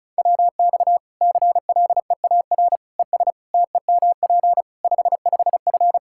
Morse code, heard as a keyed tone, about 36 wpm, 700 Hz. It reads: WX CLEAR ES TEMP 55F
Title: Practice QSO - Exchange 2 - Info - 35wpm